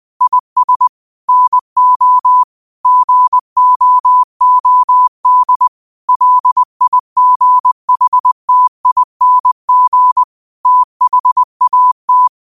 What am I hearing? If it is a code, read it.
Morse code, heard as a keyed tone, about 20 wpm, 1000 Hz. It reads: IS NO GOOD LIGHTING THAT
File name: qotdfiles / 20wpm